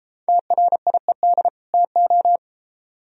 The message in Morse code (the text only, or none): TRIED TO